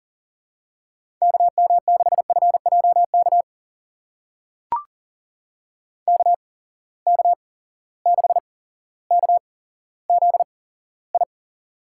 Morse code, heard as a keyed tone, about 40 wpm, spaced as for 17 wpm, 700 Hz. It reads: KM6FJK KK6KZI
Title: USA Call Signs - 6 Characters - 8x Character Spacing - 40wpm